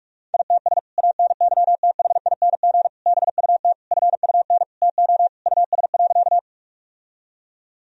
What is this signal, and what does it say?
Morse code, heard as a keyed tone, about 46 wpm, 700 Hz. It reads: ITS ANYTHING BUT FUN TO US.
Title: Sentences from Top 1000 Words - Review Part 4 - 45wpm